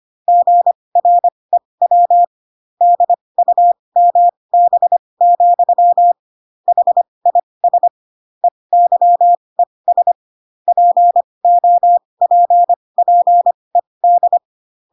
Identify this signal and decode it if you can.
Morse code, heard as a keyed tone, about 25 wpm, spaced as for 20 wpm, 700 Hz. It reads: GREW DUMB, HIS EYES POPPED
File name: Pinocchio / Pinocchio ch1 15 25 ik1hge